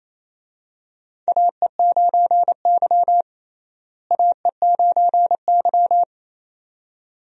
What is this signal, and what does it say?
Morse code, heard as a keyed tone, about 28 wpm, 700 Hz. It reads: AE9Y AE9Y
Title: USA Call Signs - 4 Characters - Repeated 2x - 28wpm